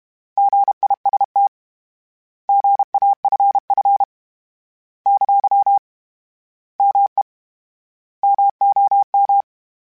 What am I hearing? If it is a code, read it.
Morse code, heard as a keyed tone, about 32 wpm, 800 Hz. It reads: GIST GAFF ! ME MOM
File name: Misc / Misc-mp3s / Misc @30